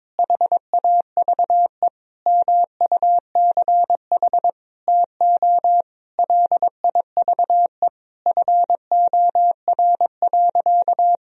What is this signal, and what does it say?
Morse code, heard as a keyed tone, about 22 wpm, 700 Hz. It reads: HAVE MUCH TO LIVE FOR.
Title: Sentences from Top 1000 Words - Review Part 3 - 22wpm